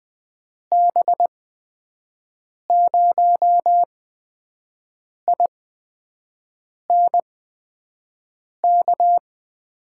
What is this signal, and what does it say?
Morse code, heard as a keyed tone, about 20 wpm, spaced as for 5 wpm, 700 Hz. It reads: B0INK
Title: USA Call Signs - 6 Characters - 8x Character Spacing - 20wpm